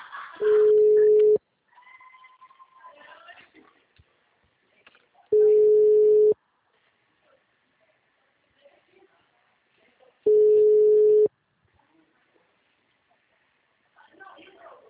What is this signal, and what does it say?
Ringback tone (Europe, 425 Hz), with speech.
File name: out- -2002- - - .56255